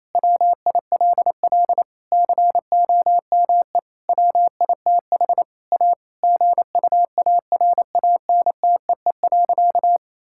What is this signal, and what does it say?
Morse code, heard as a keyed tone, about 28 wpm, 700 Hz. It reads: WILL COME WITH A GUARANTEE.
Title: Sentences from Top 1600 Words 28wpm